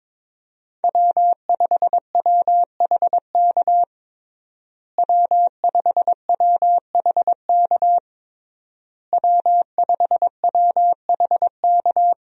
Morse code, heard as a keyed tone, about 22 wpm, 700 Hz. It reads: W5WHK W5WHK W5WHK
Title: USA Call Signs - 5 Characters - Repeated 3x - 22wpm